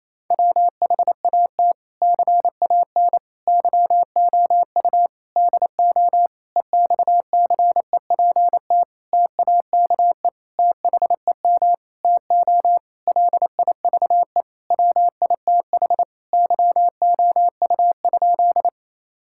Morse code, heard as a keyed tone, about 28 wpm, 700 Hz. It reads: WHAT CAN YOU DO EXCEPT TAKE THEM TO LIVE WITH YOU?
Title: Sentences from Top 1600 Words 28wpm